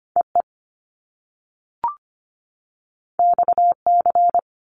Morse code, heard as a keyed tone, about 25 wpm, 700 Hz. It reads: EE XC
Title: Top Two Letter Combinations in Words 25wpm